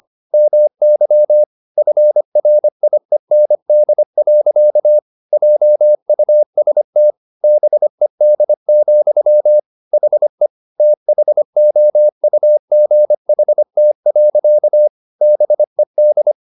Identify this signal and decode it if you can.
Morse code, heard as a keyed tone, about 25 wpm, 600 Hz. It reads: MY FRIEND. JUST BED, HE THOUGHT. BED